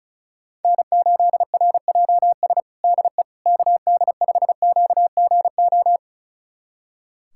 Morse code, heard as a keyed tone, about 34 wpm, 700 Hz. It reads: N8RJS DE KD5QGO
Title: USA Callsign DE Callsign 35wpm